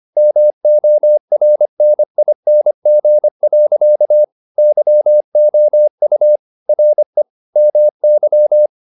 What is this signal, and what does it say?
Morse code, heard as a keyed tone, about 25 wpm, 600 Hz. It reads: MORNING. YOU RE MY
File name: the old man and the sea part 1 25wpm